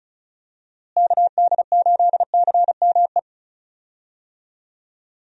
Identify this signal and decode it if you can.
Morse code, heard as a keyed tone, about 36 wpm, 700 Hz. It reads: KD8CME
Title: USA Call Signs - 6 Characters - Repeated 3x - 35wpm